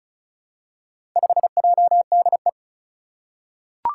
Morse code, heard as a keyed tone, about 36 wpm, 700 Hz. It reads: 5JDE
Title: Four Challenging Letter-Number Combinations 35wpm